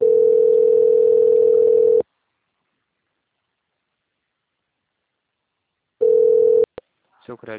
Ringback tone (North America), 440 + 480 Hz.